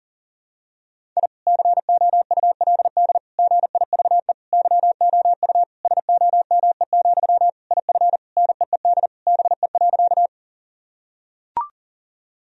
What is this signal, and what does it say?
Morse code, heard as a keyed tone, about 40 wpm, 700 Hz. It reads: I COULD GIVE YOU SOME, IF NEED BE.
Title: Sentences from Top 1000 Words - Review Part 1 - 40wpm